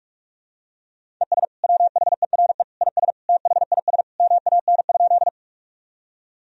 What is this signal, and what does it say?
Morse code, heard as a keyed tone, about 43 wpm, 700 Hz. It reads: ES WHERE IS THIS MAN?
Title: Sentences from Top 200 Words and Abbreviations 45wpm